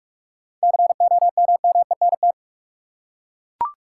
Morse code, heard as a keyed tone, about 41 wpm, 700 Hz. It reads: COMMENT
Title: Seven Letter Words - 45wpm